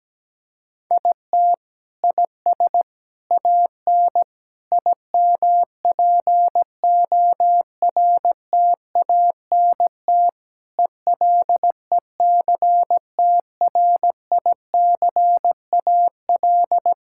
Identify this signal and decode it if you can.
Morse code, heard as a keyed tone, about 17 wpm, 700 Hz. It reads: IT IS AN IMPORTANT ELECTRICAL
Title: Sentences from Top 1000 Words - Review Part 6 - 17wpm